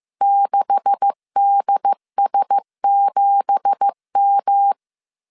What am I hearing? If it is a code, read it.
Morse code, heard as a keyed tone, about 15 wpm, 800 Hz. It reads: 6DS7M
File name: CW / DET / 1 9 1500C S A 14wpm 3222